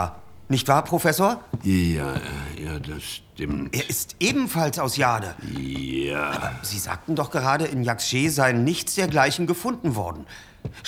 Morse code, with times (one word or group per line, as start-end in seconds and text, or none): none